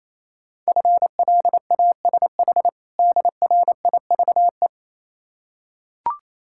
Morse code, heard as a keyed tone, about 27 wpm, 690 Hz. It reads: FLASH DRIVE
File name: Easy Words - Speed-Racing with Hints - 22wpm